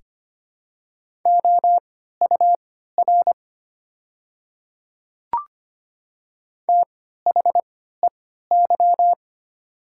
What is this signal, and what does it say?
Morse code, heard as a keyed tone, about 25 wpm, spaced as for 14 wpm, 700 Hz. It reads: OUR THEY
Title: 3x Character Spacing - Top 200 Words - 25wpm